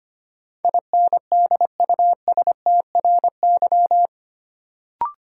Morse code, heard as a keyed tone, about 25 wpm, 700 Hz. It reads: INDUSTRY
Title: Sentences from Top 1000 Words - Review Part 4 - New Word Drill - 25wpm